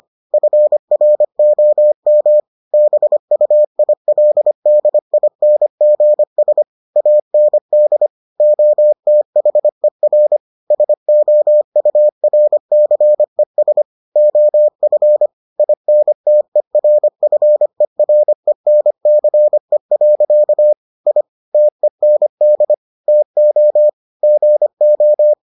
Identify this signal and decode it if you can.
Morse code, heard as a keyed tone, about 25 wpm, 600 Hz. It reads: FROM BUILDINGS AND OTHER SOURCES OF INTERFERENCE. I TEND TO GO